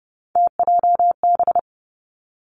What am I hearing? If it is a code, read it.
Morse code, heard as a keyed tone, about 30 wpm, 700 Hz. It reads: TJB
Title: Three Challenging Letter-Number Combinations 30wpm